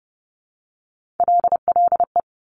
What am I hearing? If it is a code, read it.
Morse code, heard as a keyed tone, about 30 wpm, 700 Hz. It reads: LLE
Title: Top Three Letter Combinations in Words 30wpm